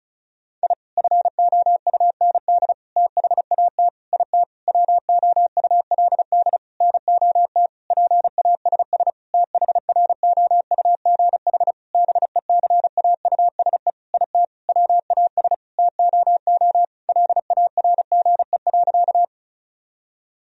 Morse code, heard as a keyed tone, about 34 wpm, 700 Hz. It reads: I FOUND THAT IT WOULD NOT PASS THROUGH BECAUSE IT WAS TOO LARGE.
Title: Sentences from Top 1000 Words - Review Part 1 - 35wpm